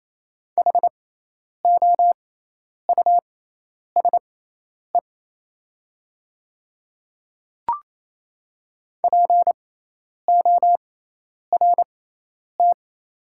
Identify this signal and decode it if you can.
Morse code, heard as a keyed tone, about 28 wpm, spaced as for 10 wpm, 700 Hz. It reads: HOUSE PORT
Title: 6x Character Spacing - Top 300 Words - 28wpm